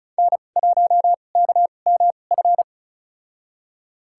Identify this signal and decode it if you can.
Morse code, heard as a keyed tone, about 36 wpm, spaced as for 25 wpm, 700 Hz. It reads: N1KMF
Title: USA Call Signs - 2x Character Spacing - 35wpm